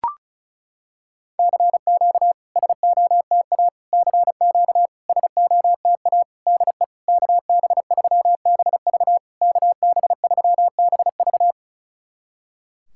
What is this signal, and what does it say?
Morse code, heard as a keyed tone, about 36 wpm, 700 Hz. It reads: CQ SOTA CQ SOTA DE KB3BV KB3BV
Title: SOTA QSO Practice 35wpm